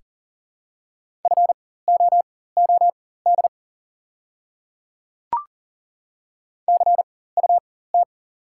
Morse code, heard as a keyed tone, about 40 wpm, spaced as for 19 wpm, 690 Hz. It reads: FOOD CUT
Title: 4x Character Spacing - Top 300 Words - 40wpm